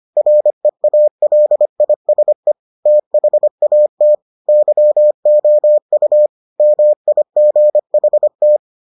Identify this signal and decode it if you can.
Morse code, heard as a keyed tone, about 25 wpm, 600 Hz. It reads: REALISE THAT YOU MIGHT